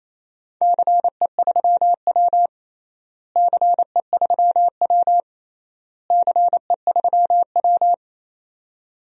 Morse code, heard as a keyed tone, about 28 wpm, 700 Hz. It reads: CE3W CE3W CE3W
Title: DXCC-100 Call Signs - 4 Characters - Repeated 3x - 28wpm